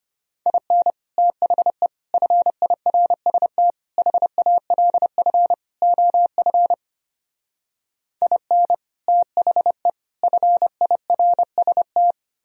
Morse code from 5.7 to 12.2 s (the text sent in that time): OF IN THE FIRST